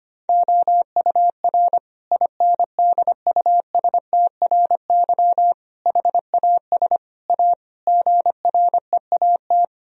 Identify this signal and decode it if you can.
Morse code, heard as a keyed tone, about 25 wpm, 700 Hz. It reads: OUR INDUSTRY HAS A GREAT
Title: Sentences from Top 1000 Words - Review Part 4 - 25wpm